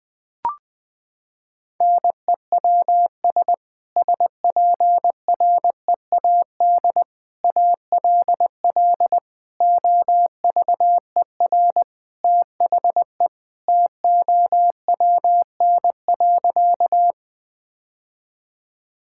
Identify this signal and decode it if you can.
Morse code, heard as a keyed tone, about 20 wpm, 700 Hz. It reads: NEWS SPREAD ALL OVER THE TOWN.
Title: Sentences with 3-Word Phrases - 20wpm